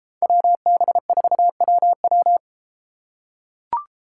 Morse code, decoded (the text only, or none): WB4WW